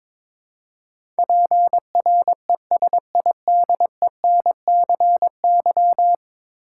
Morse code, heard as a keyed tone, about 22 wpm, 700 Hz. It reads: PRESIDENCY
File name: practice / 22wpm / 10 Letter Words - First 3 Letters Given - 22wpm